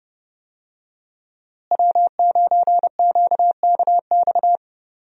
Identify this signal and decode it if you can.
Morse code, heard as a keyed tone, about 30 wpm, 700 Hz. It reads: W9QKX
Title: USA Call Signs - 5 Characters - Repeated 2x - 30wpm